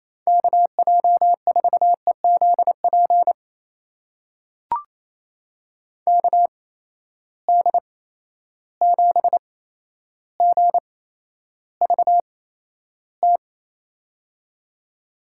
Morse code, heard as a keyed tone, about 28 wpm, spaced as for 12 wpm, 700 Hz. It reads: KJ4EZP KD7GVT
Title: USA Call Signs - 6 Characters - 8x Character Spacing - 28wpm